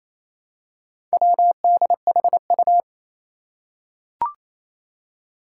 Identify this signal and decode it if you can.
Morse code, heard as a keyed tone, about 28 wpm, 700 Hz. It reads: WDHU
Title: Four Challenging Letter-Number Combinations 28wpm